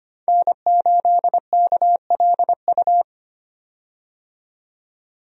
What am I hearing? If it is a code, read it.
Morse code, heard as a keyed tone, about 25 wpm, 700 Hz. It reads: N8KLU
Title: USA Call Signs - 5 Characters - 25wpm